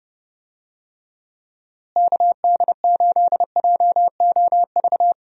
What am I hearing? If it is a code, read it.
Morse code, heard as a keyed tone, about 30 wpm, 700 Hz. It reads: KD8JOV
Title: USA Call Signs - 6 Characters - Repeated 3x - 30wpm